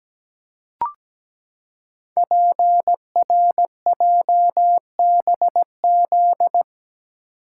Morse code, heard as a keyed tone, about 17 wpm, 700 Hz. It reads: PRJBZ
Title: Code Groups - 5 Letters - 17wpm